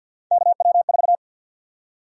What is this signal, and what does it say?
Morse code, heard as a keyed tone, about 53 wpm, 700 Hz. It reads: KW4
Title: USA Call Sign Prefixes - 50wpm